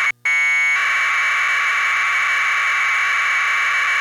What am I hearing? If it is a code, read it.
Packet radio (AX.25, as in APRS).
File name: n2ppn / VHF PKT